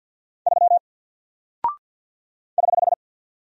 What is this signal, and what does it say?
Morse code, heard as a keyed tone, about 48 wpm, 700 Hz. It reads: 3 <HH>
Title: Single Character (Letter:Number:Punctuation:Prosign) Alt 50wpm